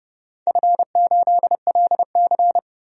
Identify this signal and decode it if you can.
Morse code, heard as a keyed tone, about 29 wpm, 700 Hz. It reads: F8LC
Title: DXCC-100 Call Signs - 4 Characters - 30wpm